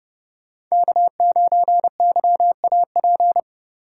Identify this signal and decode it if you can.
Morse code, heard as a keyed tone, about 39 wpm, 700 Hz. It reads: K9YAP